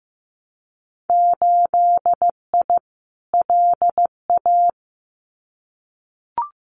Morse code, heard as a keyed tone, about 15 wpm, 700 Hz. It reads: 8I LA
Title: Winter Field Day Exchanges Encore 15wpm